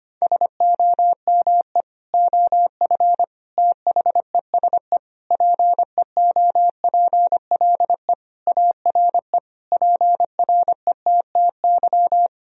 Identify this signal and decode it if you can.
Morse code, heard as a keyed tone, about 25 wpm, 700 Hz. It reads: SOME OF THESE PEOPLE ARE PRETTY
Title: Sentences from Top 1000 Words - Review Part 3 - 25wpm